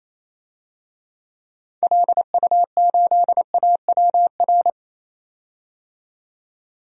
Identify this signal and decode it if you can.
Morse code, heard as a keyed tone, about 28 wpm, 700 Hz. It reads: LU8AWR
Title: DXCC-100 Call Signs - 6 Characters - 28wpm